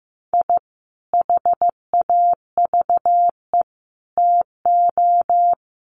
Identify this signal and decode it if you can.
Morse code, heard as a keyed tone, about 15 wpm, 700 Hz. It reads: I HAVE TO
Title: Sets of 3 Words 15wpm